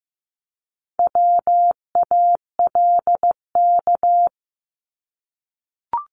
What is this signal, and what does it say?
Morse code, heard as a keyed tone, about 15 wpm, 700 Hz. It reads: WALK
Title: Top 300 Words 15wpm